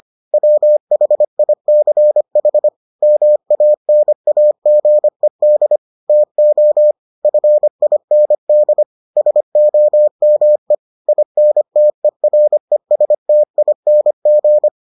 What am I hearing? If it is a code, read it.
Morse code, heard as a keyed tone, about 25 wpm, 600 Hz. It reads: WHICH MANAGED TO FIND SOME INTERESTING